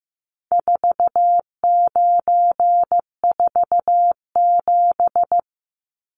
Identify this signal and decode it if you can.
Morse code, heard as a keyed tone, about 15 wpm, 700 Hz. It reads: 4947
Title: Numbers - 4-Digit 15wpm